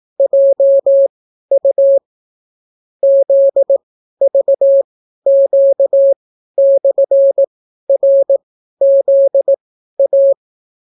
Morse code, heard as a keyed tone, about 18 wpm, spaced as for 12 wpm, 550 Hz. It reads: JU ZVQ/RZA